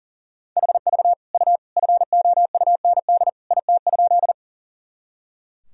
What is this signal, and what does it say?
Morse code, heard as a keyed tone, about 40 wpm, 700 Hz. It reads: HV U FOUND IT?
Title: Sentences from Top 300 Words and Abbreviations 40wpm